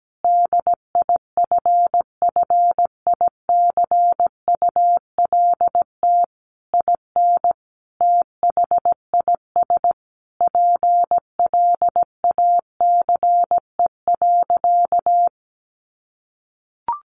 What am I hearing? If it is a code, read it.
Morse code, heard as a keyed tone, about 17 wpm, 700 Hz. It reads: DIFFICULT IN THIS PLACE.
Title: Sentences from Top 1500 Words 17wpm